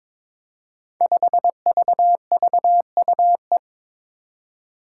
Morse code, heard as a keyed tone, about 22 wpm, 700 Hz. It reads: 5VVUE
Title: Five Challenging Letter-Number Combinations 22wpm